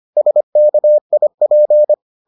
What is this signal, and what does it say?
Morse code, heard as a keyed tone, about 25 wpm, 600 Hz. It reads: SKIP